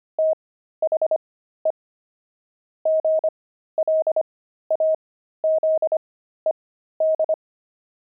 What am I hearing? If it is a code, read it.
Morse code, heard as a keyed tone, about 25 wpm, spaced as for 13 wpm, 630 Hz. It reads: THE GLAZED
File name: pangram-fs-13-cs-25-pavement